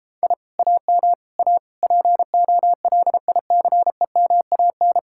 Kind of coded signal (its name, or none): Morse code